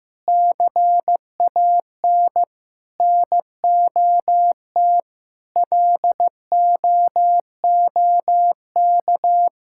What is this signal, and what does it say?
Morse code, heard as a keyed tone, about 15 wpm, 700 Hz. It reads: CAN NOT LOOK
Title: Sentences from Top 1000 Words - Review Part 6 - 15wpm